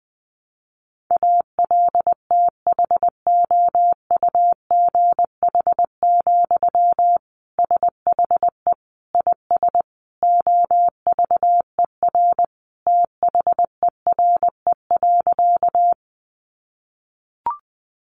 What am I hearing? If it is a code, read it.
Morse code, heard as a keyed tone, about 20 wpm, 700 Hz. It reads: ALTHOUGH, SHE IS OVER THERE.
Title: Sentences from Top 600 Words 20wpm